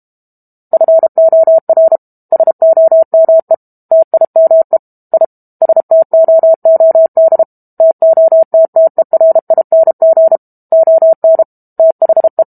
Morse code, heard as a keyed tone, about 32 wpm, 650 Hz. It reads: FOR SOME TIME I STOOD TOTTERING ON THE
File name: Book Two / 06-Chapter 6 - 32 WPM